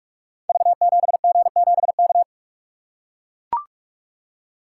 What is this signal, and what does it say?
Morse code, heard as a keyed tone, about 44 wpm, 700 Hz. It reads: V7G7K
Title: Code Groups - 5 Letter:Numbers - 45wpm